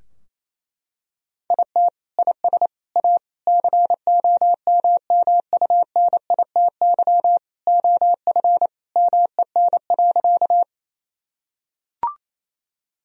Morse code, heard as a keyed tone, about 28 wpm, 700 Hz. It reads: IT IS A COMMUNITY OF MEN.
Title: Sentences from Top 1000 Words - Review Part 4 - 28wpm